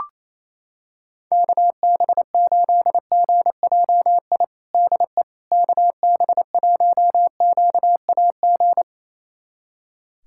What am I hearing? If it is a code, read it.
Morse code, heard as a keyed tone, about 28 wpm, 700 Hz. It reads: KB8GJI DE KB1QAG
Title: USA Callsign DE Callsign 28wpm